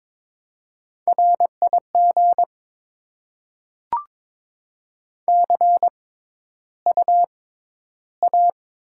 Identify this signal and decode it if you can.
Morse code, heard as a keyed tone, about 22 wpm, spaced as for 11 wpm, 700 Hz. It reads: RIG CUA
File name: Top 20 QSO Elements - 6x Character Spacing - 22wpm